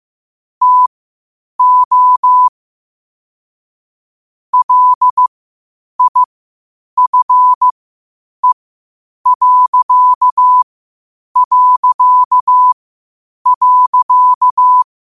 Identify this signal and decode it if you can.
Morse code, heard as a keyed tone, about 15 wpm, spaced as for 8 wpm, 1000 Hz. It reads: TO LIFE...
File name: qotdfiles / 8wpm